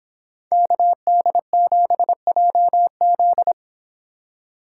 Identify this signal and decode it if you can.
Morse code, heard as a keyed tone, about 26 wpm, 700 Hz. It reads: KD7JZ